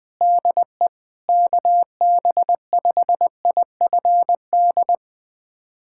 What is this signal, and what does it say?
Morse code, heard as a keyed tone, about 21 wpm, 700 Hz. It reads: DE KB5IFD
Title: USA Callsign DE Callsign 20wpm